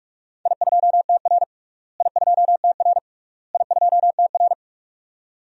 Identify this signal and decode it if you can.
Morse code, heard as a keyed tone, about 46 wpm, 700 Hz. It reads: I2TR I2TR I2TR
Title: DXCC-100 Call Signs - 4 Characters - Repeated 3x - 45wpm